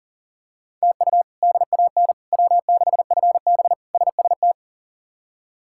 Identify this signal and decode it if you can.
Morse code, heard as a keyed tone, about 40 wpm, 700 Hz. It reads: TU DAN W6FB SST
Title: K1USN SST QSO Practice - 40wpm